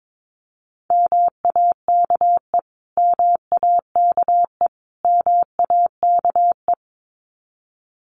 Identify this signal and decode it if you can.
Morse code, heard as a keyed tone, about 22 wpm, 700 Hz. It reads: MAKE MAKE MAKE
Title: Top 500 Words Repeated 22wpm